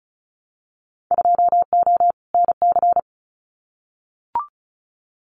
Morse code, heard as a keyed tone, about 36 wpm, 700 Hz. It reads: 2O NC
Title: Winter Field Day Exchanges 35wpm